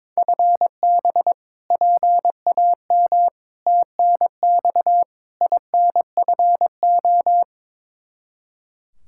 Morse code, heard as a keyed tone, about 22 wpm, 700 Hz. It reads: FB PAM TNX INFO
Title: K1USN SST QSO Practice 22wpm